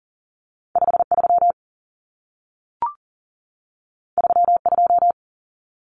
Morse code, heard as a keyed tone, about 40 wpm, 700 Hz. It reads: 53 32
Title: Common Cut Numbers - 2-Digit 40wpm